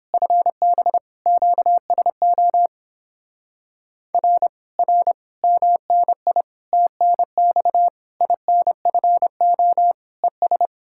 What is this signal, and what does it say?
Morse code, heard as a keyed tone, about 31 wpm, 700 Hz. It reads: FB QSO R R MNI TNX INFO ES